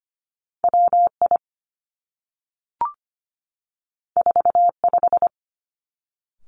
Morse code, heard as a keyed tone, about 25 wpm, 700 Hz. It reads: W I 4 5
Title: Two Challenging Letter-Number Combinations 25wpm